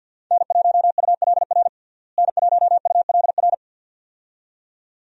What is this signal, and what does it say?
Morse code, heard as a keyed tone, about 52 wpm, 700 Hz. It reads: N1ULR N1ULR
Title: USA Call Signs - 5 Characters - Repeated 2x - 50wpm